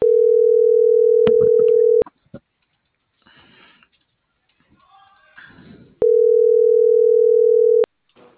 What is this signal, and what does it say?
Ringback tone (North America), 440 + 480 Hz.
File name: CENTRAL AGUILERA / out ram.lmorena.2025 - .16814